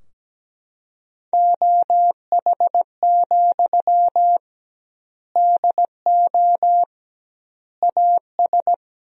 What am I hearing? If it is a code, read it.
Morse code, heard as a keyed tone, about 17 wpm, 700 Hz. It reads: OH, DO AS
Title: 2x Word-Spacing - Sentences from Top 500 Words - 17wpm